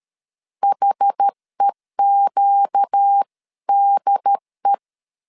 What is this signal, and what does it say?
Morse code, heard as a keyed tone, about 13 wpm, 800 Hz. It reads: HEQDE
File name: CW / DET / 1 6 1500C S A 12wpm 3440